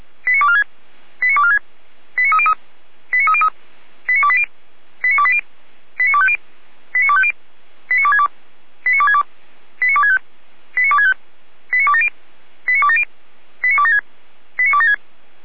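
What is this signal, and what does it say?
Five-tone selective call (ZVEI), heard as quick runs of five tones, tones of about 70 ms. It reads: 89246 89246 89303 89303 89280 89280 89250 89250 89262 89262 89265 89265 89270 89270 89276 89276
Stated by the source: ZVEI five-tone selective call received by SDR.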